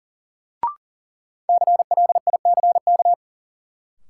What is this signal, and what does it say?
Morse code, heard as a keyed tone, about 40 wpm, 700 Hz. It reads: CLICK
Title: Sentences from Top 1100 Words - New Word Drill Alt - 40wpm